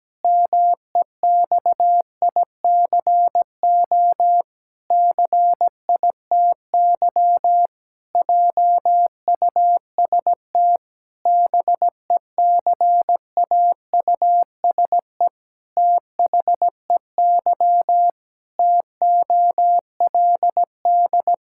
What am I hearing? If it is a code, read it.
Morse code, heard as a keyed tone, about 17 wpm, 700 Hz. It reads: MEXICO CITY JUST BECAUSE THEY TOLD
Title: Pangram Challenge v2 - 17wpm